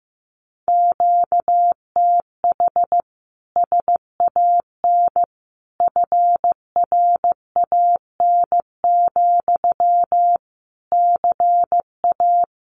Morse code, heard as a keyed tone, about 15 wpm, 700 Hz. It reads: QTH SAN FRAN, CA
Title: Common QSO Phrases Encore 15wpm